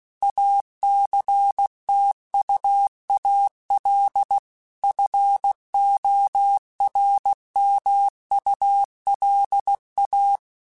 Morse code, heard as a keyed tone, about 16 wpm, 800 Hz. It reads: ACTUAL FORMULA
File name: Technical Subjects / Technical Subjects-mp3s / XL @15